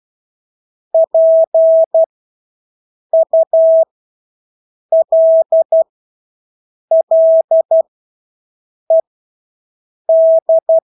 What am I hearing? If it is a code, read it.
Morse code, heard as a keyed tone, about 12 wpm, spaced as for 6 wpm, 650 Hz. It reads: PULLED
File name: Engelske ord 12-6 wpm